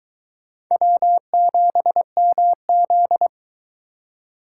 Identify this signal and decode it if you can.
Morse code, heard as a keyed tone, about 22 wpm, 700 Hz. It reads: W7MZ